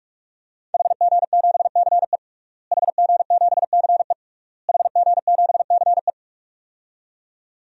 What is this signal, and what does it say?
Morse code, heard as a keyed tone, about 44 wpm, 700 Hz. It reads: HG7CE HG7CE HG7CE
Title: DXCC-100 Call Signs - 5 Characters - Repeated 3x - 45wpm